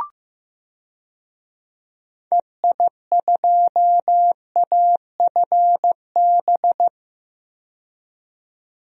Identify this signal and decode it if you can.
Morse code, heard as a keyed tone, about 15 wpm, 700 Hz. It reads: EI2AFB
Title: DXCC-100 Call Signs - 6 Characters - 15wpm